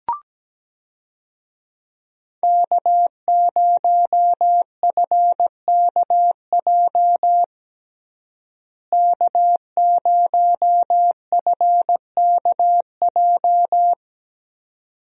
Morse code, heard as a keyed tone, about 17 wpm, 700 Hz. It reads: K0FKJ K0FKJ
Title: USA Call Signs - 5 Characters - Repeated 2x - 17wpm